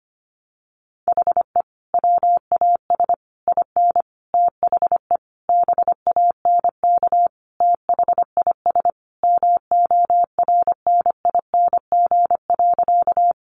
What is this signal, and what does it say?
Morse code, heard as a keyed tone, about 25 wpm, 700 Hz. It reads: HE WAS IN THE BANK THIS MORNING.
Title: Sentences from Top 900 Words 25wpm